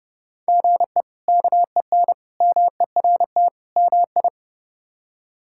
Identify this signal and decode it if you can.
Morse code, heard as a keyed tone, about 30 wpm, 700 Hz. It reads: GE KEN MERT MI
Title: K1USN SST QSO Practice - 30wpm